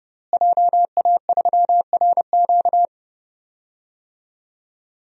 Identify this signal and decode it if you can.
Morse code, heard as a keyed tone, about 31 wpm, 700 Hz. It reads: JA3RQ
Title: DXCC-100 Call Signs - 5 Characters - Repeated 3x - 30wpm